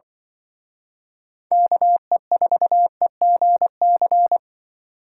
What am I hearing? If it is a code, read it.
Morse code, heard as a keyed tone, about 23 wpm, 700 Hz. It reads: KE4EGC
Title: Speed Racing - USA 2x3 Call Signs 20wpm